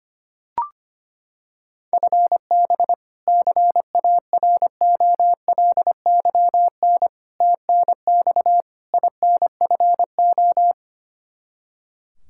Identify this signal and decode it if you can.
Morse code, heard as a keyed tone, about 25 wpm, 700 Hz. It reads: FB CAROLYN TNX INFO
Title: Practice QSO - Exchange 3 - More Info - 25wpm